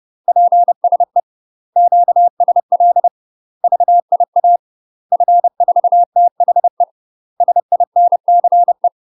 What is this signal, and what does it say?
Morse code, heard as a keyed tone, about 30 wpm, 700 Hz. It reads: PSE QSL VIA F4THE SINCE
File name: UK MASTER QSOs TRAINING / 30WPM / UK MASTER QSOs 0022